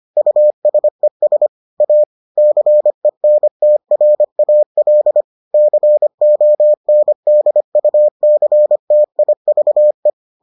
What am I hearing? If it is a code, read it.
Morse code, heard as a keyed tone, about 25 wpm, 600 Hz. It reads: USES A CENTRAL CONDUCTIVE